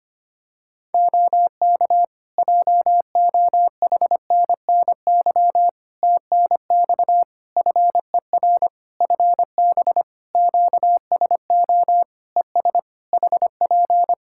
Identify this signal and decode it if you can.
Morse code, heard as a keyed tone, about 25 wpm, 700 Hz. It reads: OK JOHNNY TNX FER FB QSO ES HP
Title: Practice QSO - Exchange 4 - Ending - 25wpm